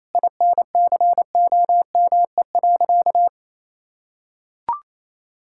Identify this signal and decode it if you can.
Morse code, heard as a keyed tone, about 27 wpm, 700 Hz. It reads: INCOME.
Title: Sentences from Top 1000 Words - Review Part 6 - 28wpm